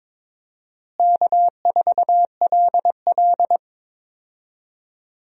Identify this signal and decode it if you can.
Morse code, heard as a keyed tone, about 22 wpm, 700 Hz. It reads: K4LL
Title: USA Call Signs - 4 Characters - Repeated 3x - 22wpm